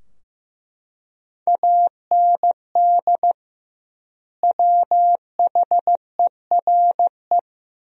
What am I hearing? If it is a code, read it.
Morse code, heard as a keyed tone, about 15 wpm, 700 Hz. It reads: AND WHERE
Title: 2x Word-Spacing - Sentences from Top 200 Words - 15wpm